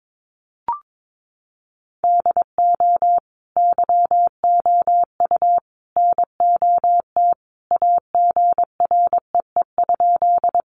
Morse code, heard as a keyed tone, about 22 wpm, 700 Hz. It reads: DO YOU NOT AGREE?
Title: Sentences from Top 1100 Words 22wpm